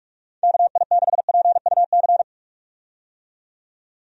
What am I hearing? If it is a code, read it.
Morse code, heard as a keyed tone, about 46 wpm, 700 Hz. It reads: KI6PUC
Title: USA Call Signs - 6 Characters - 45wpm